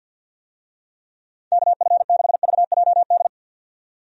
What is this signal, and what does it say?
Morse code, heard as a keyed tone, about 51 wpm, 700 Hz. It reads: KF6VJD
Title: USA Call Signs - 6 Characters - 50wpm